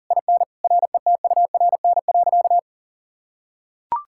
Morse code, heard as a keyed tone, about 40 wpm, 700 Hz. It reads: IN RETURN.
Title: Sentences from Top 1000 Words - Review Part 6 - 40wpm